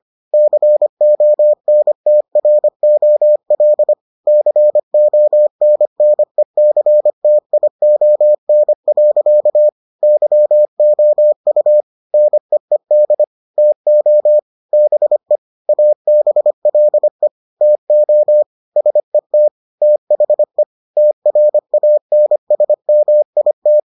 Morse code, heard as a keyed tone, about 25 wpm, 600 Hz. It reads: CONTROL CONNECTION. YOU NEED TO BE ABLE TO SET THE TRANSMIT